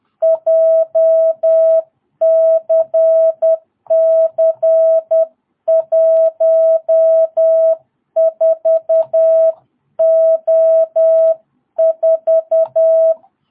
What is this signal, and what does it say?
Morse code, heard as a keyed tone, about 10 wpm, 660 Hz. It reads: JCC14O4
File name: jh8cbh / 2022cw